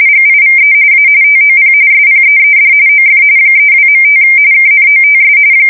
RTTY, about 45.45 baud. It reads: GIVE A TOP SPEED OF 32.5 KNOTS
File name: fsk170 / USS-Missouri